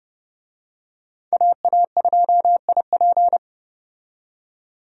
Morse code, heard as a keyed tone, about 30 wpm, 700 Hz. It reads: AA2IP
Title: USA Call Signs - 5 Characters - 30wpm